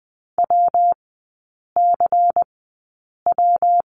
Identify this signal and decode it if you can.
Morse code, heard as a keyed tone, about 20 wpm, spaced as for 8 wpm, 700 Hz. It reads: WCW
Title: 218 Lesson 35e - Number 0 - Three Characters 20wpm